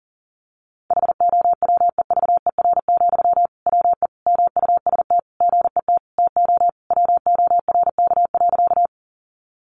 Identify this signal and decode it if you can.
Morse code, heard as a keyed tone, about 40 wpm, 700 Hz. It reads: HOWEVER, WE MUST GET TO WORK.
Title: Sentences from Top 600 Words 40wpm